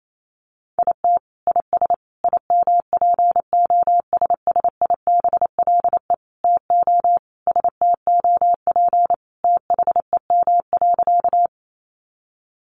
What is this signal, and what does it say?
Morse code, heard as a keyed tone, about 28 wpm, 700 Hz. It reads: IT IS IMPOSSIBLE TO STOP THEM.
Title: Sentences from Top 1100 Words 28wpm